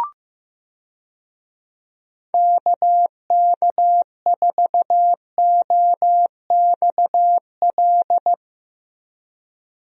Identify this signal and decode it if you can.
Morse code, heard as a keyed tone, about 15 wpm, 700 Hz. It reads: KK4OXL
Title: USA Call Signs - 6 Characters - Repeated 2x - 15wpm